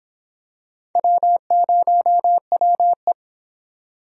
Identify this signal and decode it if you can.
Morse code, heard as a keyed tone, about 26 wpm, 700 Hz. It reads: W0WE